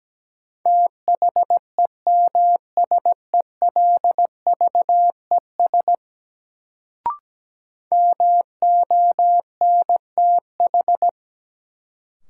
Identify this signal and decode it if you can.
Morse code, heard as a keyed tone, about 17 wpm, 700 Hz. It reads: THEMSELVES MONTH
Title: Sentences from Top 800 Words - New Word Drill - 17wpm